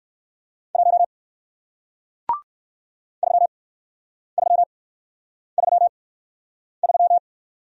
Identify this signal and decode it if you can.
Morse code, heard as a keyed tone, about 57 wpm, spaced as for 9 wpm, 700 Hz. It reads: 1 3333